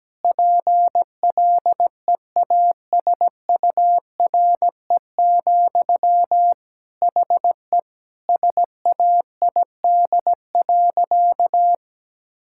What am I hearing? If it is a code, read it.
Morse code, heard as a keyed tone, about 17 wpm, 700 Hz. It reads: PLEASURE, HE SAID.
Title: Sentences from Top 1600 Words 17wpm